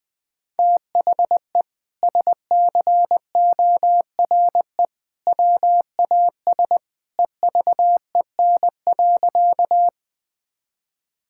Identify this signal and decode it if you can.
Morse code, heard as a keyed tone, about 21 wpm, 700 Hz. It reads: THE SCORE WAS EVEN.
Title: Sentences from Top 1600 Words 20wpm